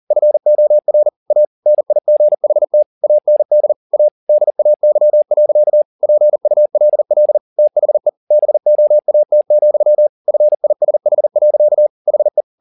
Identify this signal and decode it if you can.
Morse code, heard as a keyed tone, about 40 wpm, 600 Hz. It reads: FOR A NIGHT AND A DAY. PULL THE BOAT, FISH. HE
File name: the old man and the sea 2 40wpm